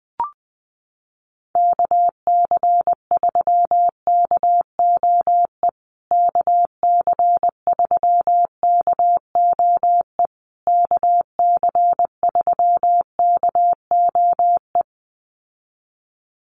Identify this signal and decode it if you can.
Morse code, heard as a keyed tone, about 20 wpm, 700 Hz. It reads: KC3KOE KC3KOE KC3KOE
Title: USA Call Signs Repeated 3x 20wpm